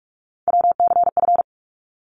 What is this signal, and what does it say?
Morse code, heard as a keyed tone, about 44 wpm, 700 Hz. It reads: WCF
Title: ARRL Sections 45wpm